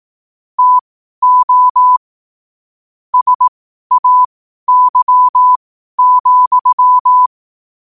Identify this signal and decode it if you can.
Morse code, heard as a keyed tone, about 18 wpm, spaced as for 12 wpm, 1000 Hz. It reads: TO SAY,
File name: qotdfiles / 12wpm